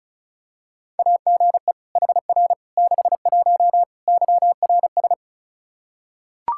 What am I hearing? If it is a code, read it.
Morse code, heard as a keyed tone, about 34 wpm, 700 Hz. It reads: AGE HR 61 YRS
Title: Practice QSO - Exchange 3 - More Info - 35wpm